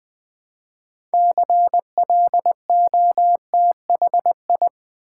Morse code, heard as a keyed tone, about 20 wpm, 700 Hz. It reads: CLOTHI
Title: Build A Word - 3 Letters At A Time - 8-Letter Words - 20wpm